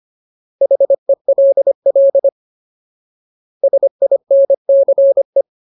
Morse code, heard as a keyed tone, about 25 wpm, 550 Hz. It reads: HELL SINCE
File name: Ultimate Word List / 25WPM Ultimate